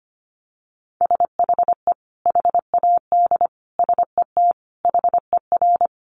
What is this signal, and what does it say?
Morse code, heard as a keyed tone, about 25 wpm, 700 Hz. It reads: SHE HAD SET HER
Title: Sentences from Top 1200 Words 25wpm